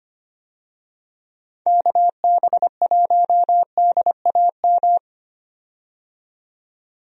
Morse code, heard as a keyed tone, about 25 wpm, 700 Hz. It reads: KB1DAM
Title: USA Call Signs - 6 Characters - 25wpm